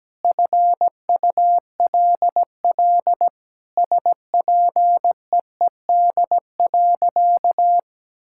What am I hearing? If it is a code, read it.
Morse code, heard as a keyed tone, about 17 wpm, 700 Hz. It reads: FULL SPEED.
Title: Sentences from Top 1000 Words - Review Part 5 - 17wpm